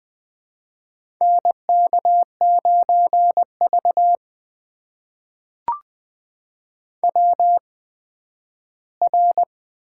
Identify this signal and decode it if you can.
Morse code, heard as a keyed tone, about 20 wpm, spaced as for 12 wpm, 700 Hz. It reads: NK9V WR
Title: USA Call Signs - 4 Characters - 8x Character Spacing - 20wpm